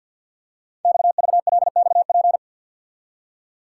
Morse code, heard as a keyed tone, about 50 wpm, 690 Hz. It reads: XVLXP
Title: Code Groups - 5 Letters - 50wpm